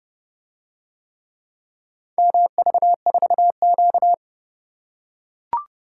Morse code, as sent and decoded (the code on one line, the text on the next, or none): -- ...- ....- --.-
MV4Q